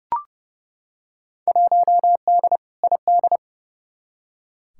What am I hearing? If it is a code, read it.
Morse code, heard as a keyed tone, about 30 wpm, 700 Hz. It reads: 1D ID
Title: Field Day Exchanges - 30wpm